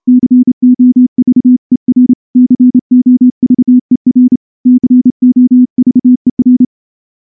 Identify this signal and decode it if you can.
Morse code, heard as a keyed tone, about 27 wpm, 250 Hz. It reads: COVER COVER COVER
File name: morse-training-30WPM